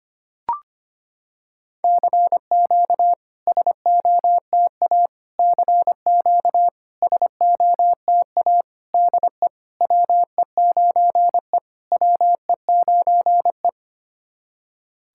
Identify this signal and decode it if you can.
Morse code, heard as a keyed tone, about 25 wpm, 700 Hz. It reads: CQ SOTA CQ SOTA DE WE9E WE9E
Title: SOTA QSO Practice 25wpm